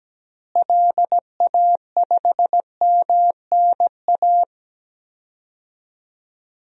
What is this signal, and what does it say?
Morse code, heard as a keyed tone, about 17 wpm, 700 Hz. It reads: LA5MNA
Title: DXCC-100 Call Signs - 6 Characters - Repeated 2x - 17wpm